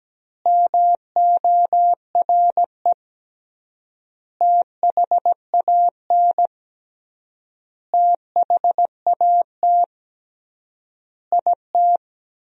Morse code, heard as a keyed tone, about 17 wpm, 700 Hz. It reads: MORE THAN THAT IT
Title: 3x Word-Spacing - Sentences from Top 200 Words - 17wpm